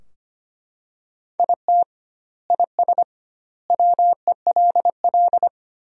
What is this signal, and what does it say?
Morse code, heard as a keyed tone, about 25 wpm, 700 Hz. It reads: IT IS WELL
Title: 2x Word-Spacing - Sentences from Top 300 Words - 25wpm